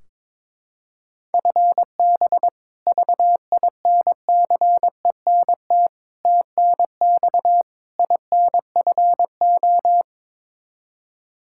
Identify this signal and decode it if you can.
Morse code, heard as a keyed tone, about 22 wpm, 700 Hz. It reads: FB VINCENT TNX INFO
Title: Practice QSO - Exchange 3 - More Info - 22wpm